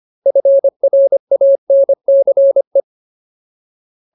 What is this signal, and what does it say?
Morse code, heard as a keyed tone, about 25 wpm, 550 Hz. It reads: FRANCE
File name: Ultimate Word List / 25WPM Ultimate